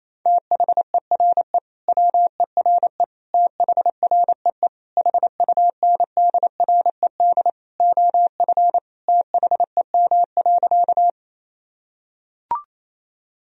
Morse code, heard as a keyed tone, about 28 wpm, 700 Hz. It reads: THERE WERE THREE HUNDRED OF THEM.
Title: Sentences from Top 1000 Words - Review Part 2 - 28wpm